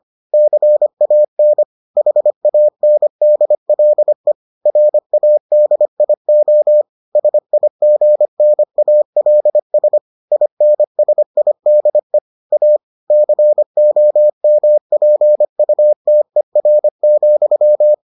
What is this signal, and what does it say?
Morse code, heard as a keyed tone, about 25 wpm, 600 Hz. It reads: CAN HANDLE RADIO SIGNALS INSIDE A COMPUTER,